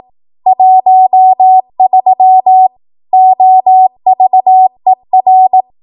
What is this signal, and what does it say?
Morse code, heard as a keyed tone, about 18 wpm, 750 Hz. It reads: 13 OVER
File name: Archive / 18 WPM / 18WPM